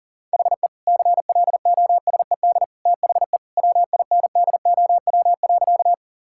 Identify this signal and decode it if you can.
Morse code, heard as a keyed tone, about 40 wpm, 700 Hz. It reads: HE CLOSED THE WINDOW.
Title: Sentences from Top 1600 Words 40wpm